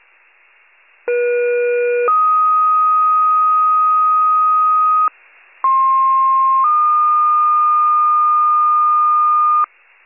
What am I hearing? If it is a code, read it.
Two-tone page, heard as a one-second tone, then a three-second tone, about 480 then 1250 Hz, about 1060 then 1220 Hz. It is heard from 1.1 to 9.7 s.
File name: wcalert